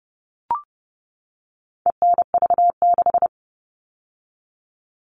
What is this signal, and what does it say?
Morse code, heard as a keyed tone, about 30 wpm, 700 Hz. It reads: ENV6
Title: All Cut Numbers - 4-Digit 30wpm